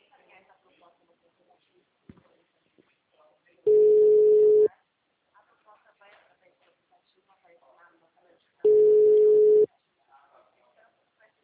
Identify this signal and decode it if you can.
Ringback tone (Europe, 425 Hz), with speech.